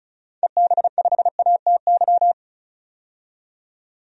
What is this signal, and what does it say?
Morse code, heard as a keyed tone, about 36 wpm, 700 Hz. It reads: EB5ATY
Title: DXCC-100 Call Signs - 6 Characters - 35wpm